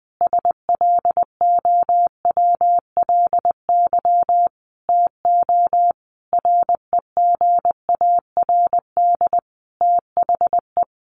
Morse code, heard as a keyed tone, about 20 wpm, 700 Hz. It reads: SLOWLY TO REGARD THE
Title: A Sound Of Thunder 20wpm